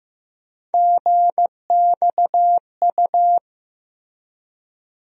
Morse code, heard as a keyed tone, about 15 wpm, 700 Hz. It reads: GXU
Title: USA Call Sign Postfixes - 3 Letters 15wpm